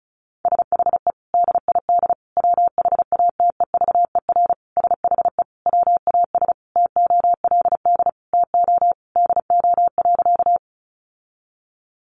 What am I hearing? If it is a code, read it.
Morse code, heard as a keyed tone, about 34 wpm, 700 Hz. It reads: SHE DID WHATEVER SHE WAS TOLD TO DO.
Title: Sentences from Top 1100 Words 35wpm